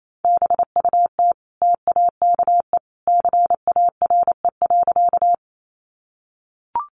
Morse code, heard as a keyed tone, about 28 wpm, 700 Hz. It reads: BUT TAKE CARE.
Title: Sentences from Top 300 Words 28wpm